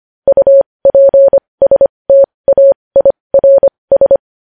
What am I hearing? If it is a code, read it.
Morse code, heard as a keyed tone, about 25 wpm, spaced as for 20 wpm, 550 Hz. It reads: UPSTAIRS
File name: podcast / News headlines-1x-25x20